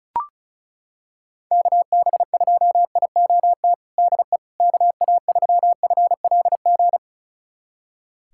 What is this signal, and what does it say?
Morse code, heard as a keyed tone, about 36 wpm, 700 Hz. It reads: KB2IOT DE KA3FLG
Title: USA Callsign DE Callsign - 6 Characters - 35wpm